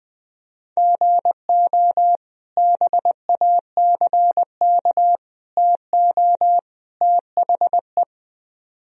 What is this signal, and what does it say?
Morse code, heard as a keyed tone, about 20 wpm, 700 Hz. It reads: GO BACK TO THE
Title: Speed-Racing - Sets of 4 Words 20wpm